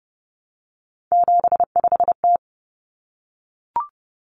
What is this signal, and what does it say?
Morse code, heard as a keyed tone, about 30 wpm, 700 Hz. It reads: 75T
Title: Common Cut Numbers - 3-Digit 30wpm